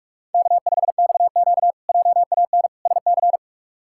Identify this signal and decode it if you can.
Morse code, heard as a keyed tone, about 41 wpm, 700 Hz. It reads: K5XQ JAN SC
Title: K1USN SST QSO Practice - 45wpm